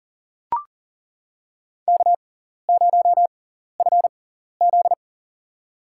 Morse code, heard as a keyed tone, about 40 wpm, spaced as for 14 wpm, 700 Hz. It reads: K0FZ
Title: USA Call Signs - 4 Characters - 6x Character Spacing - 40wpm